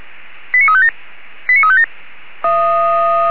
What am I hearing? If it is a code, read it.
Five-tone selective call (ZVEI), heard as quick runs of five tones, tones of about 70 ms. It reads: 89357 89357